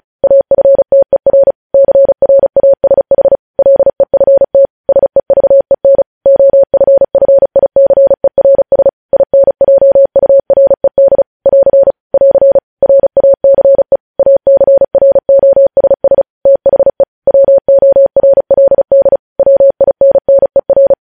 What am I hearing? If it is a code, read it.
Morse code, heard as a keyed tone, about 36 wpm, 550 Hz. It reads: AFTER CRASH LEFT SEVEN OFFICERS INJURED <AR> <AR> RACE ACROSS THE WORLD WINNER
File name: podcast / News headlines-1x-35x35